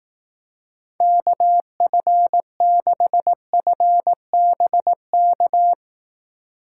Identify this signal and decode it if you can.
Morse code, heard as a keyed tone, about 18 wpm, 700 Hz. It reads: KF6FBK
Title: Speed Racing - USA 2x3 Call Signs 15wpm